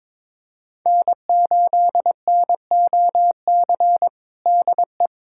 Morse code, heard as a keyed tone, about 22 wpm, 700 Hz. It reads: N8NOC DE
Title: USA Callsign DE Callsign - 5 Characters - 22wpm